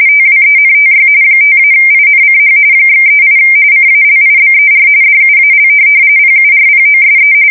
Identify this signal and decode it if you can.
RTTY, about 45.45 baud. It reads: TWENTY 5-INCH /38 CALIBER DUAL PURPOSE